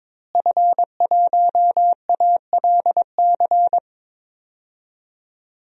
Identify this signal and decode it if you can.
Morse code, heard as a keyed tone, about 22 wpm, 700 Hz. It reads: F1ALC
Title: DXCC-100 Call Signs - 5 Characters - Repeated 3x - 22wpm